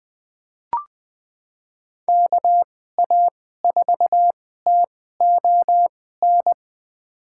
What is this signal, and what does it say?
Morse code, heard as a keyed tone, about 20 wpm, spaced as for 15 wpm, 700 Hz. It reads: KA4TON
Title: USA Call Signs - 6 Characters - 2x Character Spacing - 20wpm